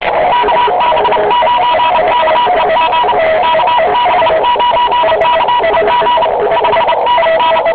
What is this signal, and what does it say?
Morse code, heard as a keyed tone, about 30 wpm, 970 Hz. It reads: KB9KEG KB9KEG 5NN
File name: j7 vp2m 2002 / Voice / j75et 15 cw